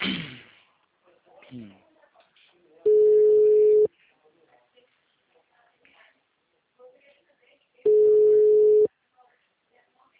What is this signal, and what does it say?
Ringback tone (Europe, 425 Hz), with speech.